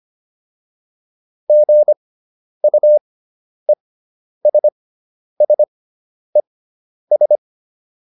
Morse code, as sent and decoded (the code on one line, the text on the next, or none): --. ..- . ... ... ...
GUESSES